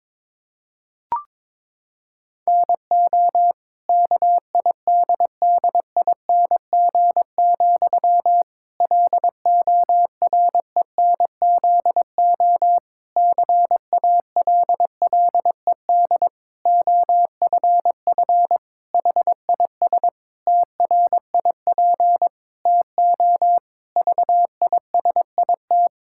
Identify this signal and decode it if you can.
Morse code, heard as a keyed tone, about 22 wpm, 700 Hz. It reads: NO KIDDING, LORENZO CALLED OFF HIS TRIP TO VISIT
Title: Pangram Challenge v2 - 22wpm